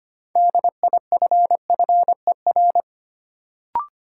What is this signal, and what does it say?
Morse code, heard as a keyed tone, about 25 wpm, 700 Hz. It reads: DIFFER
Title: Sentences from Top 1100 Words - New Word Drill Alt - 25wpm